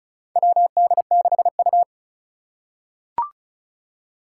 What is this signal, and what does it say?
Morse code, heard as a keyed tone, about 34 wpm, 700 Hz. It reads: WD6U
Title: USA Call Signs - 4 Characters - 2x Character Spacing - 35wpm